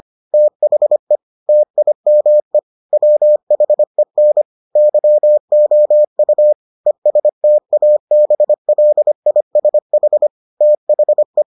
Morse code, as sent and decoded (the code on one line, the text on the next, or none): - .... / - .. -- . / .-- .... -. / -.-- --- ..- / . ... - .- -... .-.. .. ... .... / - .... .
THE TIME WHEN YOU ESTABLISH THE